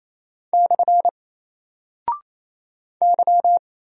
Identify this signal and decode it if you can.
Morse code, heard as a keyed tone, about 27 wpm, 700 Hz. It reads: / Y
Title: Single Character (Letter:Number:Punctuation:Prosign) 28wpm